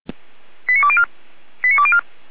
Five-tone selective call (ZVEI), heard as quick runs of five tones, tones of about 70 ms. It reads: 89204 89204